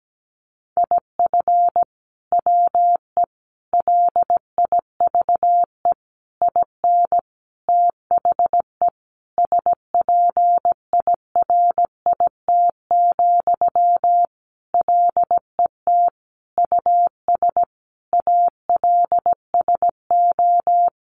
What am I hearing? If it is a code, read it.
Morse code, heard as a keyed tone, about 17 wpm, 700 Hz. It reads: IF WE LIVE IN THE SPIRIT, LET US ALSO
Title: Sentences from Top 1400 Words 17wpm